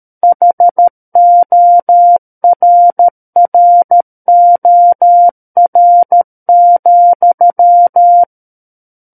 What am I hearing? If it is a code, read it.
Morse code, heard as a keyed tone, about 13 wpm, 700 Hz. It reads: HORROR,
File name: Book One / 04-Chapter 4 - 13 WPM